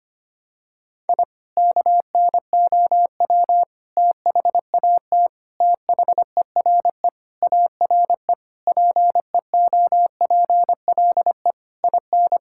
Morse code, heard as a keyed tone, about 25 wpm, 700 Hz. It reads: I KNOW THAT THERE ARE PEOPLE IN
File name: practice / 25wpm / Sentences with 3-Word Phrases - 25wpm